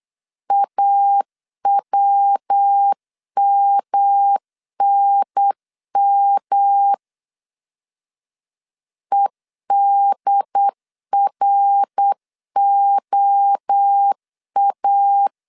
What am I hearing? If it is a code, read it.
Morse code, heard as a keyed tone, about 8 wpm, 800 Hz. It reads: AWMNM EDROA